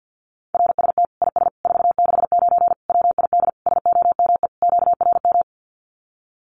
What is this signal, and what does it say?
Morse code, heard as a keyed tone, about 50 wpm, 700 Hz. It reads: RST IS 469 WID SOME QRM
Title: Common QSO Phrases 50wpm